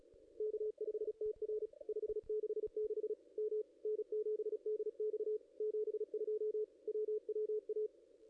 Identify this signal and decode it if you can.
Morse code, heard as a keyed tone, about 33 wpm, 440 Hz. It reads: K5TR 566 M N7DX 72 WWA
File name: contests / ss-cw-k5tr-op-n5rz -0400